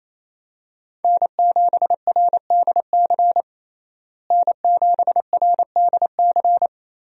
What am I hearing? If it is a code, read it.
Morse code, heard as a keyed tone, about 28 wpm, 700 Hz. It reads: N7RDC N7RDC
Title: USA Call Signs - 5 Characters - Repeated 2x - 28wpm